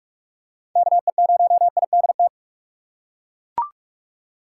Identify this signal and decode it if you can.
Morse code, heard as a keyed tone, about 41 wpm, 700 Hz. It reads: KE0IDT